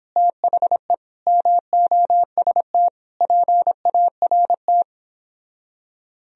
Morse code, heard as a keyed tone, about 29 wpm, 700 Hz. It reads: THE MOST PART